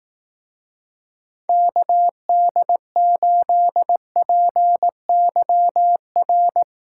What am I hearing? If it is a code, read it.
Morse code, heard as a keyed tone, about 18 wpm, 700 Hz. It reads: KD8PYR
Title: Speed Racing - USA 2x3 Call Signs 15wpm